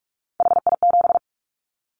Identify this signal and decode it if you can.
Morse code, heard as a keyed tone, about 44 wpm, 700 Hz. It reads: HI7
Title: Callsign Prefixes for Top 100 DXCC - 45wpm